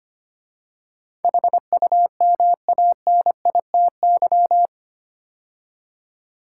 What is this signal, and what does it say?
Morse code, heard as a keyed tone, about 25 wpm, 700 Hz. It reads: HUMANITY
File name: 8 Letter Words - First 3 Letters Given - 25wpm